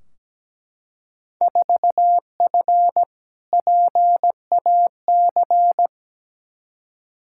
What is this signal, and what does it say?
Morse code, heard as a keyed tone, about 17 wpm, 700 Hz. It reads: 4F PAC
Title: Field Day Exchanges Encore - 17wpm